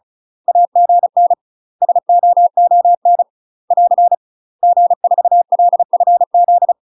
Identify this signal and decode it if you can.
Morse code, heard as a keyed tone, about 34 wpm, 700 Hz. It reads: AGN SOON <AR> G4LFZ
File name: UK MASTER QSOs TRAINING / 35WPM / UK MASTER QSOs 0003